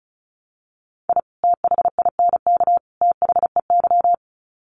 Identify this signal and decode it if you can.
Morse code, heard as a keyed tone, about 34 wpm, 700 Hz. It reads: I THINK THEY
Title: Sets of 3 Words 35wpm